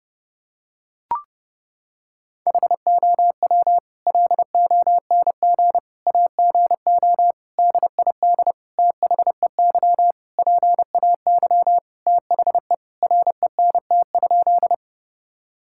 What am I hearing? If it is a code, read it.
Morse code, heard as a keyed tone, about 32 wpm, 700 Hz. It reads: HOW LONG AGO DID THEY PAY THE RENT?
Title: Sentences from Top 1000 Words - Review Part 5 - 30wpm